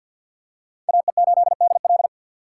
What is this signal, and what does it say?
Morse code, heard as a keyed tone, about 53 wpm, 700 Hz. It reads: AE8DL
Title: USA Call Signs - 5 Characters - 4x Character Spacing - 50wpm